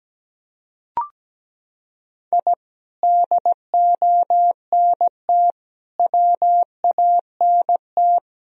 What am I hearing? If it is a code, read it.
Morse code, heard as a keyed tone, about 17 wpm, 700 Hz. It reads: I DONT WANT
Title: Sentences from Top 1000 Words - Review Part 1 - 17wpm